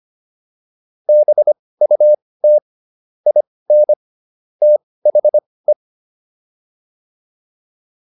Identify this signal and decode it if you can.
Morse code, heard as a keyed tone, about 25 wpm, spaced as for 18 wpm, 600 Hz. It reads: BUT IN THE